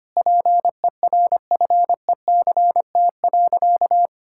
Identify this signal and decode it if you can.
Morse code, heard as a keyed tone, about 25 wpm, 700 Hz. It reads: PERFECT.
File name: practice / 25wpm / Sentences with 3-Word Phrases - 25wpm